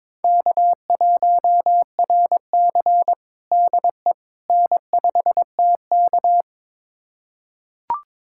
Morse code, heard as a keyed tone, about 22 wpm, 700 Hz. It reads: K1RC DE N5TK
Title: USA Callsign DE Callsign - 4 Characters - 22wpm